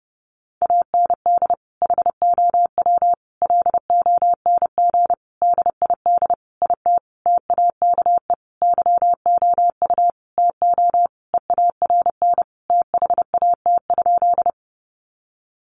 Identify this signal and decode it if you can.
Morse code, heard as a keyed tone, about 30 wpm, 700 Hz. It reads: AND HOW LONG DID IT TAKE YOU TO EARN THAT?
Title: Sentences from Top 1500 Words 30wpm